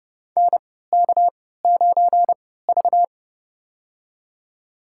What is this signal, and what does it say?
Morse code, heard as a keyed tone, about 32 wpm, spaced as for 17 wpm, 700 Hz. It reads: NK9V
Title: USA Call Signs - 4 Characters - 3x Character Spacing - 30wpm